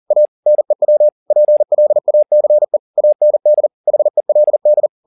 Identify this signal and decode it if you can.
Morse code, heard as a keyed tone, about 40 wpm, 600 Hz. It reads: A NEW PLACE AND HELD
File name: the old man and the sea 2 40wpm